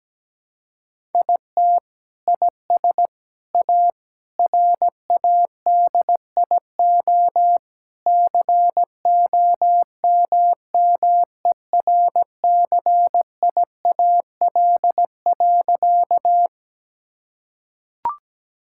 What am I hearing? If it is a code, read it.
Morse code, heard as a keyed tone, about 17 wpm, 700 Hz. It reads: IT IS A RADIO COMMERCIAL.
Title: Sentences from Top 1000 Words - Review Part 5 - 17wpm